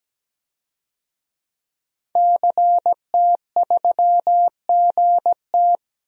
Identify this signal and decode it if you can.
Morse code, heard as a keyed tone, about 17 wpm, 700 Hz. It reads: CT3GT
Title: DXCC-100 Call Signs - 5 Characters - Repeated 2x - 17wpm